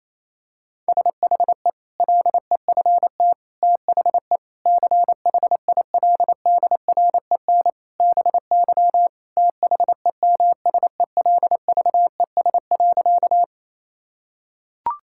Morse code, heard as a keyed tone, about 28 wpm, 700 Hz. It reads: SHE LEFT THE CHILDREN BY THEMSELVES.
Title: Sentences from Top 1000 Words - Review Part 5 - 28wpm